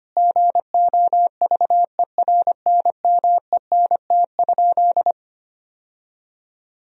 Morse code, heard as a keyed tone, about 25 wpm, 700 Hz. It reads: GOVERNMENT?
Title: Sentences from Top 1000 Words - Review Part 4 - 25wpm